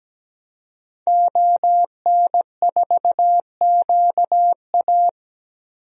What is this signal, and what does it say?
Morse code, heard as a keyed tone, about 17 wpm, 700 Hz. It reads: ON4QA
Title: DXCC-100 Call Signs - 5 Characters - Repeated 2x - 17wpm